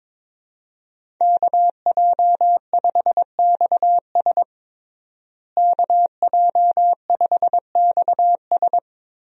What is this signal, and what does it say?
Morse code, heard as a keyed tone, about 22 wpm, 700 Hz. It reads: KJ5XS KJ5XS
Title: USA Call Signs - 5 Characters - Repeated 2x - 22wpm